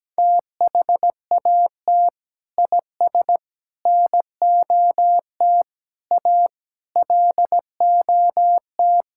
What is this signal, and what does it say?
Morse code, heard as a keyed tone, about 17 wpm, 700 Hz. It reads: THAT IS NOT A LOT
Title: Sentences from Top 1000 Words - Review Part 5 - 17wpm